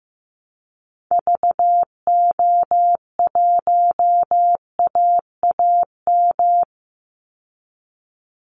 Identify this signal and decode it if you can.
Morse code, heard as a keyed tone, about 15 wpm, 700 Hz. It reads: VO1AAM
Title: Call Signs Repeated 2x - 15wpm